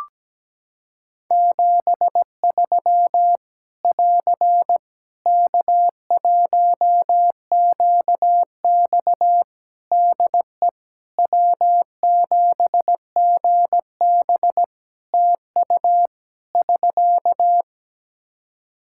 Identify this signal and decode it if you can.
Morse code, heard as a keyed tone, about 17 wpm, 700 Hz. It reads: 73 <AR> K1QX DE W7GB TU <SK>
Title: K1USN SST QSO Practice 17wpm